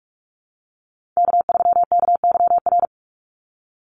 Morse code, heard as a keyed tone, about 44 wpm, 700 Hz. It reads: K3KYR
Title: NA SOTA Chaser Callsigns 45wpm